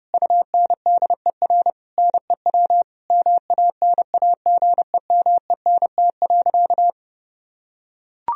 Morse code, heard as a keyed tone, about 30 wpm, 700 Hz. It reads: UNDER NEW MANAGEMENT.
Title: Sentences from Top 1000 Words - Review Part 4 - 30wpm